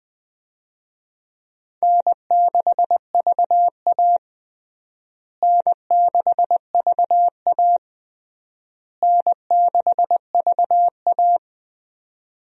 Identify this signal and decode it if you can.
Morse code, heard as a keyed tone, about 20 wpm, 700 Hz. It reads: N6VA N6VA N6VA
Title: USA Call Signs - 4 Characters - Repeated 3x - 20wpm